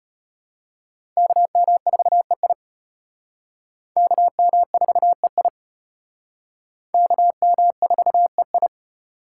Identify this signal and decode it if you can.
Morse code, heard as a keyed tone, about 31 wpm, 700 Hz. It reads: KM4EI KM4EI KM4EI
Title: Speed Racing - USA 2x2 Call Signs 25wpm